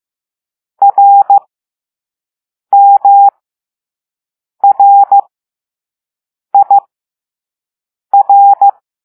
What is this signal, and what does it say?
Morse code, heard as a keyed tone, about 15 wpm, spaced as for 5 wpm, 800 Hz. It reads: RMRIR
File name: Morse / Level2A